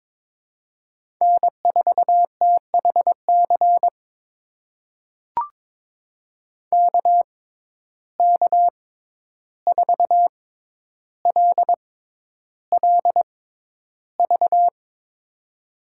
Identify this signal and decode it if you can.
Morse code, heard as a keyed tone, about 22 wpm, spaced as for 11 wpm, 700 Hz. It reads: N4THC KK4LLV
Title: USA Call Signs - 6x Character Spacing - 22wpm